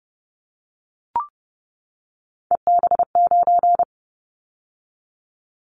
Morse code, heard as a keyed tone, about 31 wpm, 700 Hz. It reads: EB9
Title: Callsign Prefixes for Top 100 DXCC - 30wpm